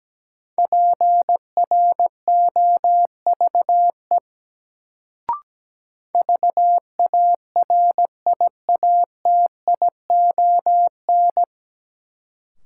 Morse code, heard as a keyed tone, about 17 wpm, 700 Hz. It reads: PROVE VARIATION
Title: Sentences from Top 1600 Words - New Word Drill Alt - 17wpm